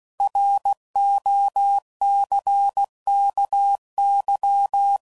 Morse code, heard as a keyed tone, about 16 wpm, 800 Hz. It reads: ROCKY
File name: CW Code Groups @15WPM / fromSM's / Cities